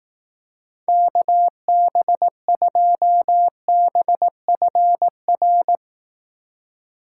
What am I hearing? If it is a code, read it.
Morse code, heard as a keyed tone, about 18 wpm, 700 Hz. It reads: KB2BFR
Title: Speed Racing - USA 2x3 Call Signs 15wpm